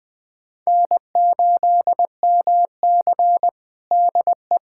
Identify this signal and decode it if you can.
Morse code, heard as a keyed tone, about 21 wpm, 700 Hz. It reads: N8MC DE
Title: USA Callsign DE Callsign 20wpm